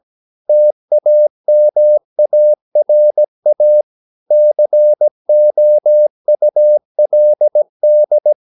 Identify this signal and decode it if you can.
Morse code, heard as a keyed tone, about 17 wpm, 600 Hz. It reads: TAMARA COULD
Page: Romanen Never af Ken Follet på engelsk.